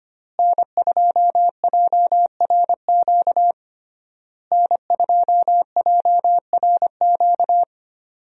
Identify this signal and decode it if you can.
Morse code, heard as a keyed tone, about 25 wpm, 700 Hz. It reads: N2JRQ N2JRQ
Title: USA Call Signs - 5 Characters - Repeated 3x - 25wpm